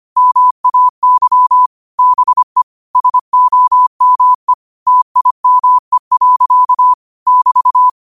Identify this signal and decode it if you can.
Morse code, heard as a keyed tone, about 25 wpm, 1000 Hz. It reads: MAY BE SOME TIME. <BT>
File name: qotdfiles / 25wpm